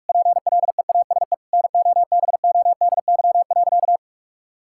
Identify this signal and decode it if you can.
Morse code, heard as a keyed tone, about 46 wpm, 700 Hz. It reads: PLEASE NOBODY.
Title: Sentences from Top 1600 Words 45wpm